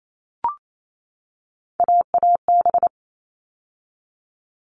Morse code, heard as a keyed tone, about 28 wpm, 700 Hz. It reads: AAB
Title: All Cut Numbers - 3-Digit 28wpm